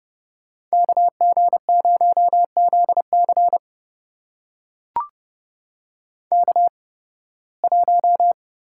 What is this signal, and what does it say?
Morse code, heard as a keyed tone, about 31 wpm, spaced as for 20 wpm, 700 Hz. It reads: KG0ZC K1
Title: USA Call Signs - 5 Characters - 8x Character Spacing - 30wpm